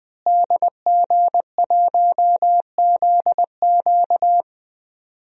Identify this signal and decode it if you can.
Morse code, heard as a keyed tone, about 20 wpm, 700 Hz. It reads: DG1ZQ
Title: DXCC-100 Call Signs - 5 Characters - Repeated 2x - 20wpm